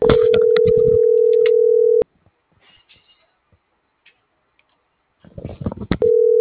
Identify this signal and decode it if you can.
Ringback tone (North America), 440 + 480 Hz.